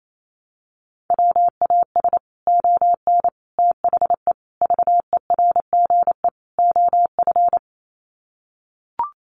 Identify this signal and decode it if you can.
Morse code, heard as a keyed tone, about 28 wpm, 700 Hz. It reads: WAS ON THE VERGE OF
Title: Sets of 5 Words Encore 28wpm